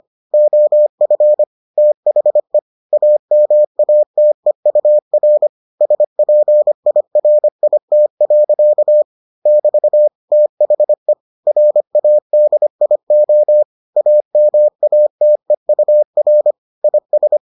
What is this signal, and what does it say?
Morse code, heard as a keyed tone, about 25 wpm, 600 Hz. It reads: OF THE AMATEUR SPIRIT. <BT> THE RADIO AMATEUR IS